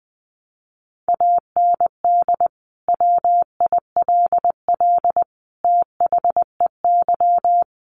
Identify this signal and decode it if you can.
Morse code, heard as a keyed tone, about 20 wpm, 700 Hz. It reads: AND WILL THEY
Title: Sentences from Top 1400 Words 20wpm